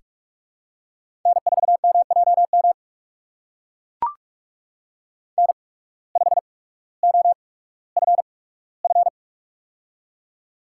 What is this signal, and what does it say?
Morse code, heard as a keyed tone, about 42 wpm, spaced as for 19 wpm, 700 Hz. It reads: N4MJM N5OFF
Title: USA Call Signs - 8x Character Spacing - 45wpm